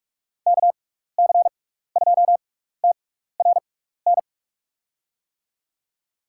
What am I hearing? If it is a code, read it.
Morse code, heard as a keyed tone, about 44 wpm, spaced as for 15 wpm, 700 Hz. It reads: KC2TRN
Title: USA Call Signs - 6 Characters - 6x Character Spacing - 45wpm